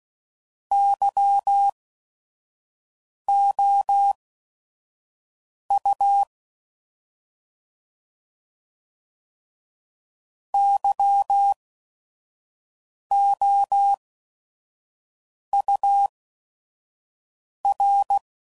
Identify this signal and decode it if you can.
Morse code, heard as a keyed tone, about 16 wpm, spaced as for 5 wpm, 800 Hz. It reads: YOU YOUR